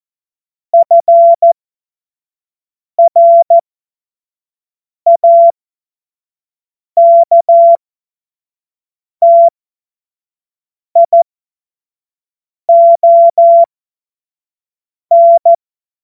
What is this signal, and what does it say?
Morse code, heard as a keyed tone, about 14 wpm, spaced as for 5 wpm, 680 Hz. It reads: FRAKTION